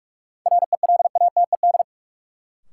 Morse code, heard as a keyed tone, about 46 wpm, 700 Hz. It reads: RELATED
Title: Sentences from Top 1000 Words - Review Part 5 - New Word Drill - 45wpm